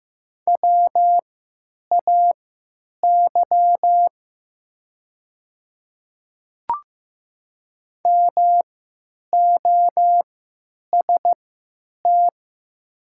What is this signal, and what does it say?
Morse code, heard as a keyed tone, about 15 wpm, spaced as for 9 wpm, 700 Hz. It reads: WAY MOST
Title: 3x Character Spacing - Top 100 Words - 15wpm